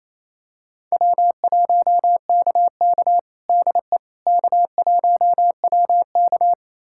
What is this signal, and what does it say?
Morse code, heard as a keyed tone, about 28 wpm, 700 Hz. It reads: W1KK DE K1WK
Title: USA Callsign DE Callsign - 4 Characters - 28wpm